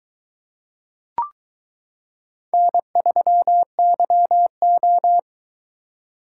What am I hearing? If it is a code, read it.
Morse code, heard as a keyed tone, about 22 wpm, 700 Hz. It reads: N3YO